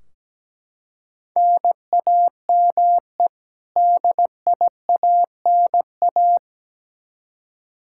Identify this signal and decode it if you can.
Morse code, heard as a keyed tone, about 17 wpm, 700 Hz. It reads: NAME DIANA
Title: Practice QSO - Exchange 1 - Beginning - 17wpm